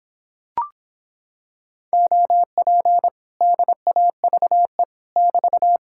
Morse code, heard as a keyed tone, about 26 wpm, 700 Hz. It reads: OP DAVE <BT>